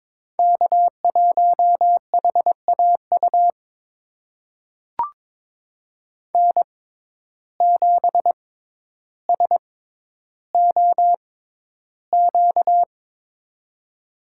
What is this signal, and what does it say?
Morse code, heard as a keyed tone, about 22 wpm, spaced as for 11 wpm, 700 Hz. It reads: K1HAU N7SOQ
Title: USA Call Signs - 5 Characters - 6x Character Spacing - 22wpm